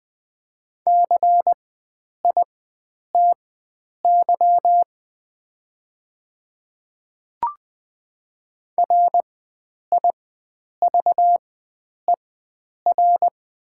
Morse code, heard as a keyed tone, about 20 wpm, spaced as for 9 wpm, 700 Hz. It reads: CITY RIVER
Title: 4x Character Spacing - Top 500 Words - 20wpm